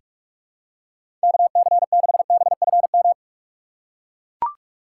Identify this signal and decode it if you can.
Morse code, heard as a keyed tone, about 41 wpm, 700 Hz. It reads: KC6BFM
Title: USA Call Signs - 6 Characters - 4x Character Spacing - 45wpm